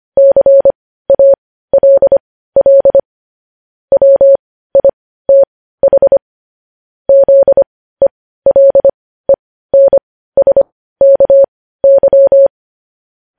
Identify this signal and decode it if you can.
Morse code, heard as a keyed tone, about 25 wpm, spaced as for 15 wpm, 550 Hz. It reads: CALL WITH ZELENSKY
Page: news headlines in morse code at 15 wpm